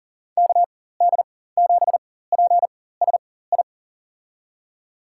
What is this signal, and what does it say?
Morse code, heard as a keyed tone, about 40 wpm, spaced as for 19 wpm, 690 Hz. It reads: KD7PSI
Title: USA Call Signs - 4x Character Spacing - 40wpm